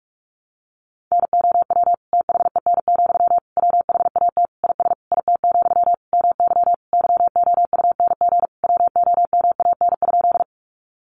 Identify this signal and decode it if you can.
Morse code, heard as a keyed tone, about 44 wpm, 700 Hz. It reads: NOW THEN, WHAT IS IT, MY YOUNG WOMAN?
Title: Sentences from Top 1100 Words 45wpm